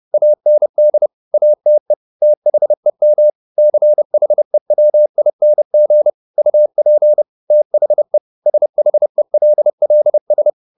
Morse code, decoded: AND ATE THEM CHEWING UP THE SHELLS

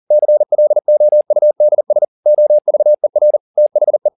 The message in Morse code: CLOUDS OVER THE

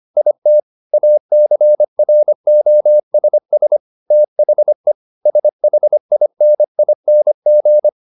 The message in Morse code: IT ACROSS THE SHINING

